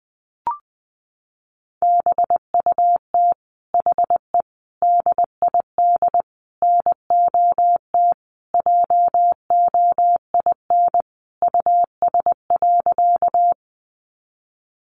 Morse code, heard as a keyed tone, about 19 wpm, 700 Hz. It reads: BUT HE DID NOT JOIN US.
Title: Sentences from Top 1400 Words 20wpm